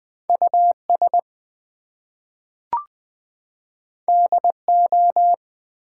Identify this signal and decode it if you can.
Morse code, heard as a keyed tone, about 20 wpm, 700 Hz. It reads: US DO
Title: Two Letter Words - 20wpm